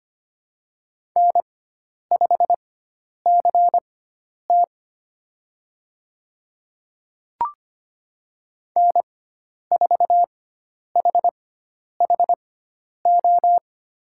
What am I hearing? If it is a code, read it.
Morse code, heard as a keyed tone, about 25 wpm, spaced as for 10 wpm, 700 Hz. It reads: N5CT N4HHO
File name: Lesson 16g - Letter c - Call Signs 25wpm